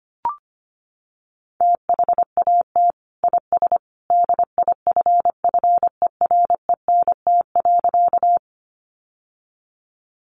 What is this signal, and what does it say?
Morse code, heard as a keyed tone, about 25 wpm, 700 Hz. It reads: THAT IS DIFFERENT.
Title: Sentences from Top 600 Words 25wpm